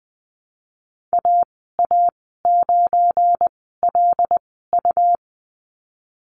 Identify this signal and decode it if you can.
Morse code, heard as a keyed tone, about 20 wpm, spaced as for 15 wpm, 700 Hz. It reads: AA9LU
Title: 200 Lesson 32g - Letter k - Call Signs 20wpm